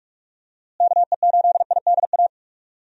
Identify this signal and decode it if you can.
Morse code, heard as a keyed tone, about 41 wpm, 700 Hz. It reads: KE8IDA
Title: USA Call Signs - 6 Characters - Repeated 3x - 45wpm